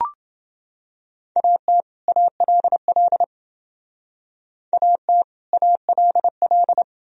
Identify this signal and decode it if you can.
Morse code, heard as a keyed tone, about 28 wpm, 700 Hz. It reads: AT ALL AT ALL